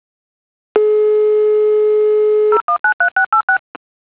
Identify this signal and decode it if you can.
DTMF, heard as telephone touch tones, tones of about 100 ms. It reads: 0193686